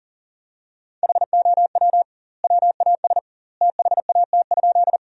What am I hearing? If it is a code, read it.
Morse code, heard as a keyed tone, about 40 wpm, 700 Hz. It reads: HOW WAS THAT?
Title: 2x Word-Spacing - Sentences from Top 100 Words - 40wpm